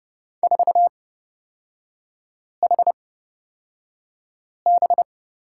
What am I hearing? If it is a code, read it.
Morse code, heard as a keyed tone, about 30 wpm, spaced as for 5 wpm, 700 Hz. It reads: 4HB
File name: Lesson 38e - Letter q - Three Characters 30wpm